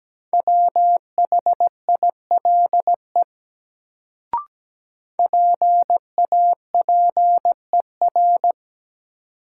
Morse code, heard as a keyed tone, about 17 wpm, 700 Hz. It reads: WHILE PAPER
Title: Sentences from Top 300 Words - New Word Drill - 17wpm